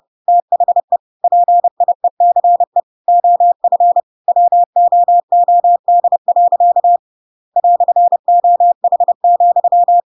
Morse code, heard as a keyed tone, about 30 wpm, 700 Hz. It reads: THE PIECE OF WOOD. "OH,
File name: Pinocchio / Pinocchio ch1 30 30 ik1hge